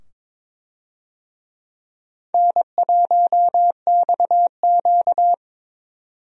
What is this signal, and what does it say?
Morse code, heard as a keyed tone, about 22 wpm, 700 Hz. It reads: N1XQ
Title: USA Call Signs - 4 Characters - 22wpm